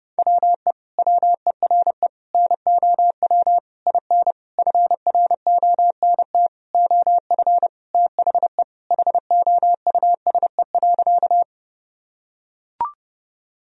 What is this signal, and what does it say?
Morse code, heard as a keyed tone, about 30 wpm, 700 Hz. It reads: WE WERE NOW IN FRONT OF THE HOUSE.
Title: Sentences from Top 1000 Words - Review Part 2 - 30wpm